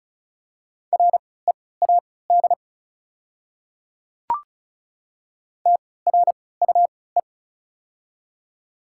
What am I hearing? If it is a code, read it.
Morse code, heard as a keyed tone, about 34 wpm, spaced as for 20 wpm, 690 Hz. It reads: READ TRUE
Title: 3x Character Spacing - Top 500 Words - 35wpm